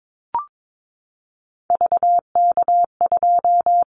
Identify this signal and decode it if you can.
Morse code, heard as a keyed tone, about 22 wpm, 700 Hz. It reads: VK2
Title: Callsign Prefixes for Top 50 DXCC - 22wpm